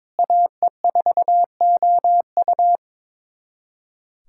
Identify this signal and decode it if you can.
Morse code, heard as a keyed tone, about 21 wpm, 700 Hz. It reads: AE4OU
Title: USA Callsign DE Callsign - 5 Characters - 22wpm